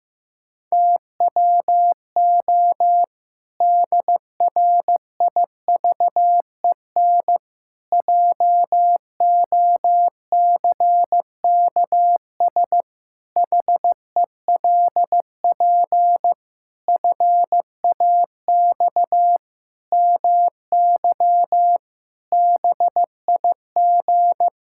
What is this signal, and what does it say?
Morse code, heard as a keyed tone, about 15 wpm, 700 Hz. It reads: TWO DRIVEN JOCKS HELP FAX MY BIG
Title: Pangram Challenge v2 - 15wpm